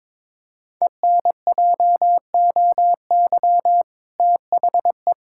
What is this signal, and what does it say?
Morse code, heard as a keyed tone, about 22 wpm, 700 Hz. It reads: ENJOY THE
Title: Sentences from Top 1000 Words - Review Part 5 - 22wpm